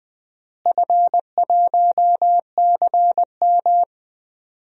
Morse code, heard as a keyed tone, about 20 wpm, 700 Hz. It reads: F1CM
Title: DXCC-100 Call Signs - 4 Characters - Repeated 2x - 20wpm